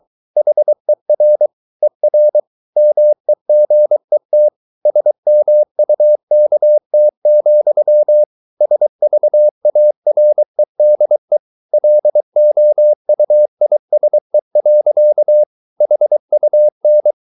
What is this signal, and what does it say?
Morse code, heard as a keyed tone, about 23 wpm, 600 Hz. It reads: HER ER MEGET SMUKT, SVAREDE LOUISE. HUN
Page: Ferie 23 wpm